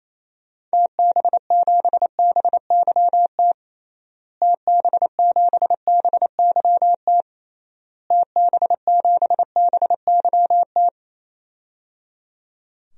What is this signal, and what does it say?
Morse code, heard as a keyed tone, about 28 wpm, 700 Hz. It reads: TB7BYT TB7BYT TB7BYT
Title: DXCC-100 Call Signs - 6 Characters - Repeated 3x - 28wpm